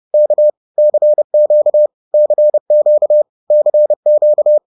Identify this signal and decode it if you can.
Morse code, heard as a keyed tone, about 30 wpm, 600 Hz. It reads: K CQ CQ CQ
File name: 30wpm